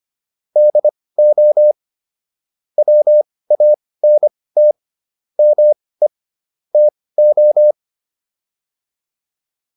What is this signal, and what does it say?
Morse code, heard as a keyed tone, about 25 wpm, spaced as for 18 wpm, 600 Hz. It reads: DO WANT ME TO
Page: intermediate-practice-files